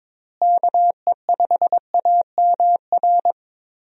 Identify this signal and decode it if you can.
Morse code, heard as a keyed tone, about 22 wpm, 700 Hz. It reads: KE5AMR
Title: USA Call Signs - 6 Characters - Repeated 3x - 22wpm